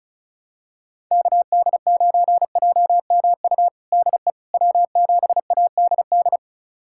Morse code, heard as a keyed tone, about 36 wpm, 700 Hz. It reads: KD9JMU DE W7ADD
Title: USA Callsign DE Callsign 35wpm